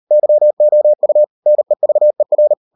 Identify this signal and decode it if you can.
Morse code, heard as a keyed tone, about 38 wpm, 600 Hz. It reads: YOU NEVER